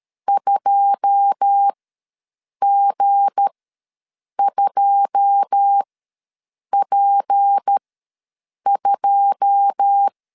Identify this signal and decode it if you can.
Morse code, heard as a keyed tone, about 13 wpm, spaced as for 7 wpm, 800 Hz. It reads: 2G2P2
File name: LEZIONI / 9 2 ws 12wpm 3406